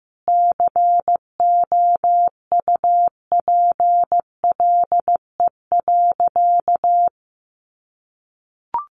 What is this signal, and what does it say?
Morse code, heard as a keyed tone, about 15 wpm, 700 Hz. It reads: COUPLE.
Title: Sentences from Top 1200 Words 15wpm